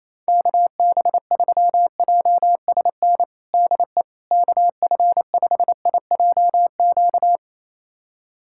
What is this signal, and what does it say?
Morse code, heard as a keyed tone, about 28 wpm, 700 Hz. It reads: KB3JSN DE KF5IJQ
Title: USA Callsign DE Callsign - 6 Characters - 28wpm